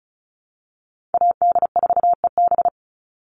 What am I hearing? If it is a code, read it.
Morse code, heard as a keyed tone, about 36 wpm, 700 Hz. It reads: AD4EB
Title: Call Signs - 35wpm